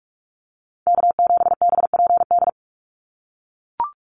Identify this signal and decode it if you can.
Morse code, heard as a keyed tone, about 44 wpm, 700 Hz. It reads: K7BPD
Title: USA Call Signs 45wpm